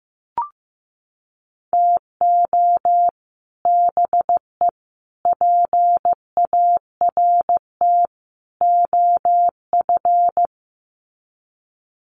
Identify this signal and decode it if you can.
Morse code, heard as a keyed tone, about 15 wpm, 700 Hz. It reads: TO BE PART OF
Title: Sets of 4 Words 15wpm